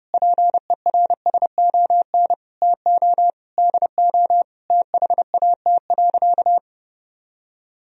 Morse code, heard as a keyed tone, about 30 wpm, 700 Hz. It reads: PERSON TO DO THAT.
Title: Sentences from Top 1000 Words - Review Part 3 - 30wpm